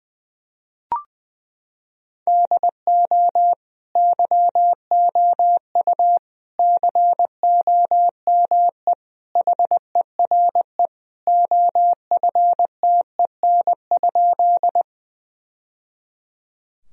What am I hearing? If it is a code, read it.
Morse code, heard as a keyed tone, about 21 wpm, 700 Hz. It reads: DO YOU COME HERE OFTEN?
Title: Sentences from Top 1000 Words - Review Part 1 - 20wpm